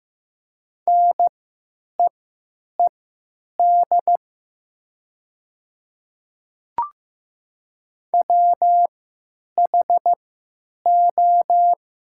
Morse code, heard as a keyed tone, about 15 wpm, spaced as for 9 wpm, 700 Hz. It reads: NEED WHO
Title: 3x Character Spacing - Top 300 Words - 15wpm